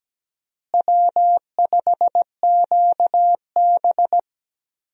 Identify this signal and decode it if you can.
Morse code, heard as a keyed tone, about 17 wpm, 700 Hz. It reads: W5QB
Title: USA Call Signs - 4 Characters - Repeated 3x - 17wpm